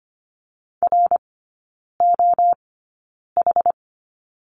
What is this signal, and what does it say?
Morse code, heard as a keyed tone, about 25 wpm, spaced as for 9 wpm, 700 Hz. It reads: ROH
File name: course / 25:10wpm / 093 Lesson 16e - Letter c - Three Characters 25wpm